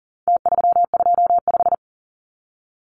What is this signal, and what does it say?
Morse code, heard as a keyed tone, about 40 wpm, 700 Hz. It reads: T325
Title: Numbers - Contest Style - 4-Digit 40wpm